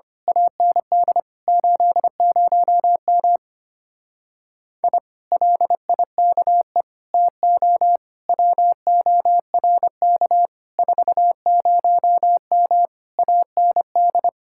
Morse code, heard as a keyed tone, about 24 wpm, 700 Hz. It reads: AND 80M I LIKE TO WORK 40M AND
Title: Speed-Racing Common QSO Phrases Encore - 25wpm